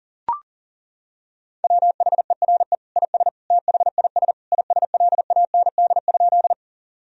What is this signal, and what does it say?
Morse code, heard as a keyed tone, about 37 wpm, 700 Hz. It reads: WHERE IS THIS ISLAND?
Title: Sentences from Top 1000 Words - Review Part 3 - 40wpm